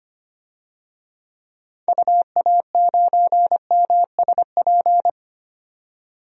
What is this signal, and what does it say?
Morse code, heard as a keyed tone, about 25 wpm, 700 Hz. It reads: UA9MSP
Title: DXCC-100 Call Signs - 6 Characters - 25wpm